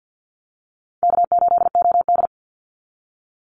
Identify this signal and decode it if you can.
Morse code, heard as a keyed tone, about 50 wpm, 700 Hz. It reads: K8OD
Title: Easy Call Signs - 50wpm